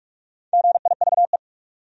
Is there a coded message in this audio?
Yes: Morse code from 0.5 to 1.4 s, heard as a keyed tone, about 43 wpm, 700 Hz. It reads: GIVE